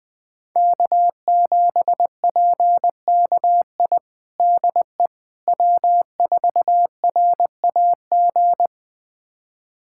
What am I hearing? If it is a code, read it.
Morse code, heard as a keyed tone, about 20 wpm, 700 Hz. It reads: K7PKI DE W4RAG
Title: USA Callsign DE Callsign - 5 Characters - 20wpm